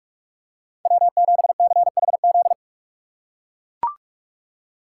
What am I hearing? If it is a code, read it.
Morse code, heard as a keyed tone, about 42 wpm, 700 Hz. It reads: W7CHZ
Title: USA Call Signs - 5 Characters - 6x Character Spacing - 45wpm